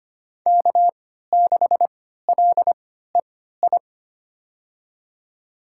Morse code, heard as a keyed tone, about 25 wpm, spaced as for 14 wpm, 700 Hz. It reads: K6LEI
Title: USA Call Signs - 5 Characters - 3x Character Spacing - 25wpm